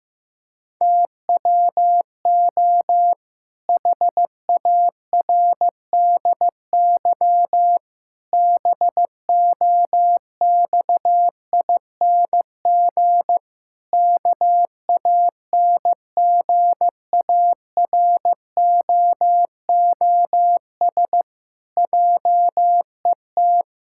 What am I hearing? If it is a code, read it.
Morse code, heard as a keyed tone, about 16 wpm, 700 Hz. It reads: TWO HARDY BOXING KANGAROOS JET
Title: Pangram Challenge v2 - 15wpm